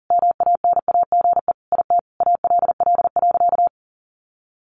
Morse code, heard as a keyed tone, about 40 wpm, 700 Hz. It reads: MANAGE IT ALL.
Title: Sentences from Top 1400 Words 40wpm